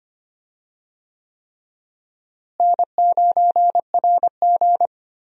Morse code, heard as a keyed tone, about 25 wpm, 700 Hz. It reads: N9RG
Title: USA Call Signs - 4 Characters - Repeated 3x - 25wpm